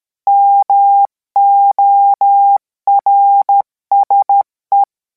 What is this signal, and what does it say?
Morse code, heard as a keyed tone, about 13 wpm, 800 Hz. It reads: MORSE